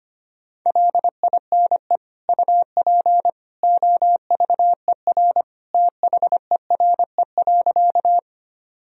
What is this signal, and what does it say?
Morse code, heard as a keyed tone, about 25 wpm, 700 Hz. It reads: LINE UP OVER THERE.
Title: Sentences from Top 1000 Words - Review Part 1 - 25wpm